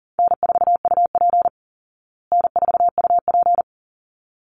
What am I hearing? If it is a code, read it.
Morse code, heard as a keyed tone, about 37 wpm, 700 Hz. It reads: N4UP N4UP
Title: Easy Call Signs Repeated 2x - 40wpm